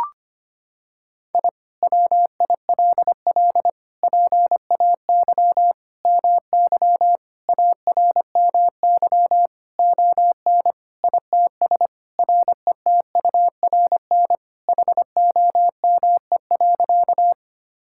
Morse code, heard as a keyed tone, about 25 wpm, 700 Hz. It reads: I WILL PAY MY ARMY ON ITS RETURN HOME.
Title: Sentences from Top 1000 Words - Review Part 5 - 25wpm